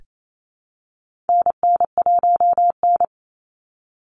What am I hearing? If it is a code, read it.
Morse code, heard as a keyed tone, about 28 wpm, 700 Hz. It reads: NN1N
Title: Call Signs Repeated 3x - 28wpm